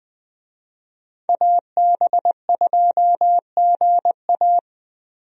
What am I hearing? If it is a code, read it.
Morse code, heard as a keyed tone, about 20 wpm, 700 Hz. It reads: AB2GA
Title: USA Call Signs - 5 Characters - 20wpm